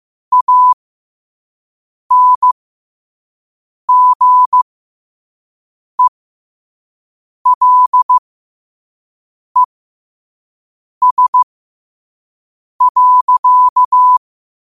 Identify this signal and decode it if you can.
Morse code, heard as a keyed tone, about 15 wpm, spaced as for 5 wpm, 1000 Hz. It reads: ANGELES.
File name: qotdfiles / 5wpm